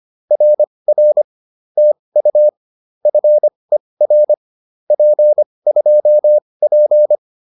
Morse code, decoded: RR TU FER P2P

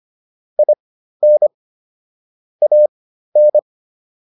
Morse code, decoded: IN AN

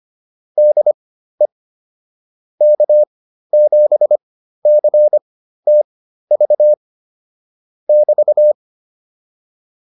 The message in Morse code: DE K7CTV <BT>

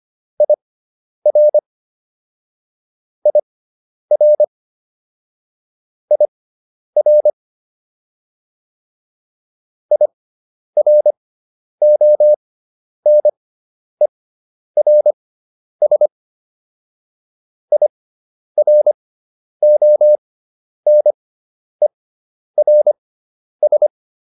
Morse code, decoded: IR IR IR IRONERS IRONERS

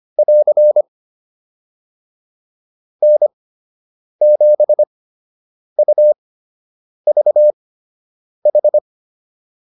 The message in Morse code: <AR> N7UVH

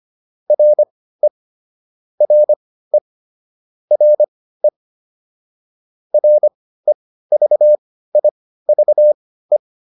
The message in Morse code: RE RE RE REVIVE

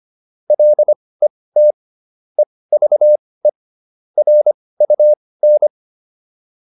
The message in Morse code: LET EVE RUN

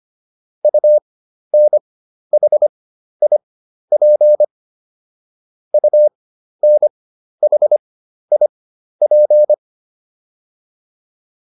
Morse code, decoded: UNHIP UNHIP

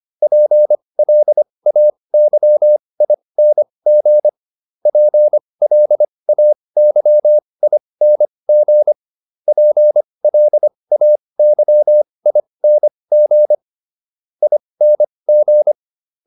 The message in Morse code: PLAYING PLAYING PLAYING ING

